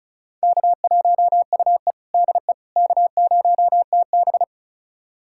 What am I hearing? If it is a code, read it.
Morse code, heard as a keyed tone, about 36 wpm, 700 Hz. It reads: K1UE DE K0TB
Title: USA Callsign DE Callsign - 4 Characters - 35wpm